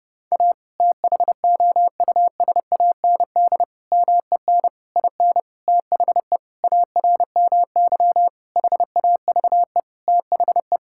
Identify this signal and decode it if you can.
Morse code, heard as a keyed tone, about 30 wpm, 700 Hz. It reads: A THOUSAND MEN IN THE ARMY HAVE THE
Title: Sentences from Top 1000 Words - Review Part 6 - 30wpm